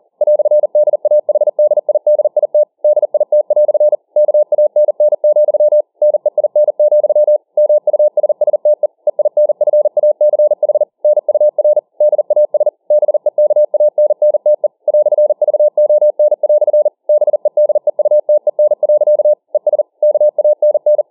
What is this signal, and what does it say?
Morse code, heard as a keyed tone, about 40 wpm, 600 Hz. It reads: "DAHDIDIT DIT" KANN, NEIN, MUSSTE EINFACH NUR DAS BEKANNTE "VON" BEDEUTEN. ES KANN